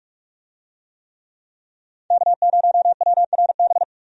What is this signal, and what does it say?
Morse code, heard as a keyed tone, about 46 wpm, 700 Hz. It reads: K0WRB
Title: USA Call Signs - 5 Characters - 45wpm